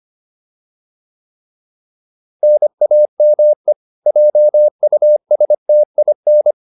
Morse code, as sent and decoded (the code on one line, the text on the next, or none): -. .- -- . / .--- ..- ... - .. -.
NAME JUSTIN